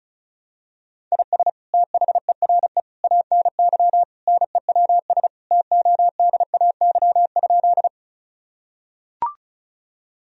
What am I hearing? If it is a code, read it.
Morse code, heard as a keyed tone, about 36 wpm, 700 Hz. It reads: IS THERE ANY NEWS TODAY?
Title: Sentences from Top 1000 Words - Review Part 4 - 35wpm